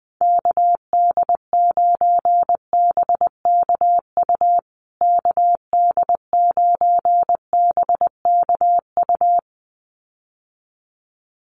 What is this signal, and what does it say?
Morse code, heard as a keyed tone, about 20 wpm, 700 Hz. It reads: KD9BKU KD9BKU
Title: USA Call Signs Repeated 2x 20wpm